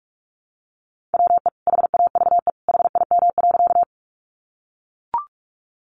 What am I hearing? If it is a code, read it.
Morse code, heard as a keyed tone, about 44 wpm, 700 Hz. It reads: WE HAVE HIM.
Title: Sentences from Top 100 Words 45wpm